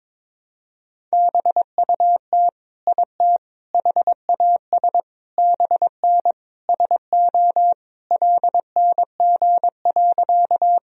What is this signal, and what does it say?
Morse code, heard as a keyed tone, about 22 wpm, 700 Hz. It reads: BUT IT HAS BN SO LNG.
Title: Sentences from Top 100 Words and Abbreviations 22wpm